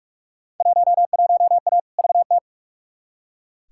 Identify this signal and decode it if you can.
Morse code, heard as a keyed tone, about 44 wpm, 700 Hz. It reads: 11A VT
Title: Field Day Exchanges Encore - 45wpm